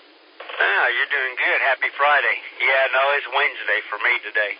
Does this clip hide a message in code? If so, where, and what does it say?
Yes: Emergency Alert System (SAME) tones from 0.5 to 4.3 s, heard as harsh buzzing bursts, with speech.